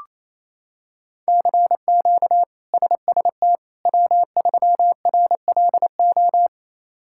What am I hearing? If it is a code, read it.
Morse code, heard as a keyed tone, about 28 wpm, 700 Hz. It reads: CQ SST W3RLO
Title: K1USN SST QSO Practice - 28wpm